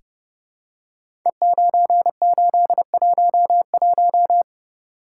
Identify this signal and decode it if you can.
Morse code, heard as a keyed tone, about 30 wpm, 700 Hz. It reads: E9811
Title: Code Groups - 5 Letter:Numbers - 30wpm